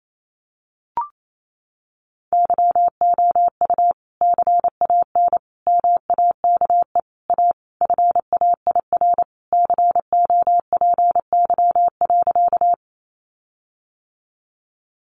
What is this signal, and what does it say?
Morse code, heard as a keyed tone, about 28 wpm, 700 Hz. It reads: YOU CAN MAKE A FAIR COPY.
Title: Sentences from Top 1400 Words 28wpm